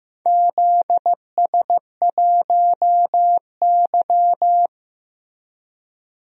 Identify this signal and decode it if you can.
Morse code, heard as a keyed tone, about 15 wpm, 700 Hz. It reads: ZS1Y
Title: DXCC-100 Call Signs - 4 Characters - Repeated 2x - 15wpm